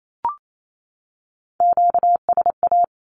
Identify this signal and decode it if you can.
Morse code, heard as a keyed tone, about 28 wpm, 700 Hz. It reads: QSA
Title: Q-Codes 28wpm